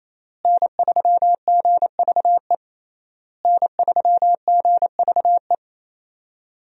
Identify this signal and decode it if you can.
Morse code, heard as a keyed tone, about 28 wpm, 700 Hz. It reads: N3GVE N3GVE
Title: USA Call Signs - 5 Characters - Repeated 3x - 28wpm